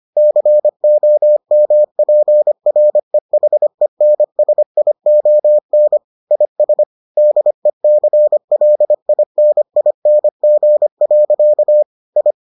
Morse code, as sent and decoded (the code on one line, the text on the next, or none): -.-. --- -- .--. .-. .... -. ... .. --- -. / .. ... / -.. -.-. .-.. .. -. .. -. --. .-.-.- / ..
COMPREHENSION IS DECLINING. I